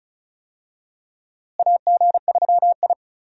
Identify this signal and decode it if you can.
Morse code, heard as a keyed tone, about 34 wpm, 700 Hz. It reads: AG3I
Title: USA Call Signs - 4 Characters - Repeated 2x - 35wpm